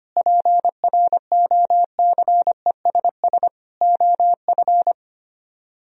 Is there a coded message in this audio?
Yes: Morse code from 0.2 to 4.9 s, heard as a keyed tone, about 33 wpm, 700 Hz. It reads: PROCESS OF